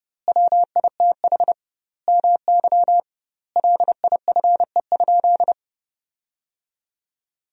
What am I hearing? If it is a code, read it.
Morse code, heard as a keyed tone, about 32 wpm, 700 Hz. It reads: WITH MY LIFE?
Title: 2x Word-Spacing - Sentences from Top 300 Words - 30wpm